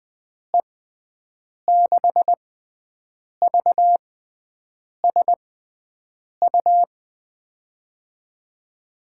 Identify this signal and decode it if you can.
Morse code, heard as a keyed tone, about 21 wpm, spaced as for 7 wpm, 700 Hz. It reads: E6VSU
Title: USA Call Signs - 6 Characters - 6x Character Spacing - 20wpm